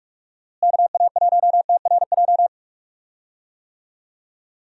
Morse code, heard as a keyed tone, about 46 wpm, 700 Hz. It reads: KA1TRJ
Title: USA Call Signs - 6 Characters - Repeated 2x - 45wpm